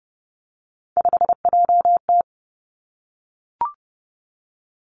Morse code, heard as a keyed tone, about 30 wpm, 700 Hz. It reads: 5JT
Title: Three Challenging Letter-Number Combinations 30wpm